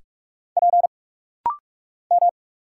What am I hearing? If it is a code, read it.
Morse code, heard as a keyed tone, about 46 wpm, 700 Hz. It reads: P M
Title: Single Letters - Rapid-Fire 45wpm